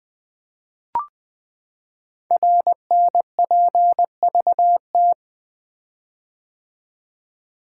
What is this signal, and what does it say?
Morse code, heard as a keyed tone, about 20 wpm, 700 Hz. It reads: RNPVT
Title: Code Groups - 5 Letters - 20wpm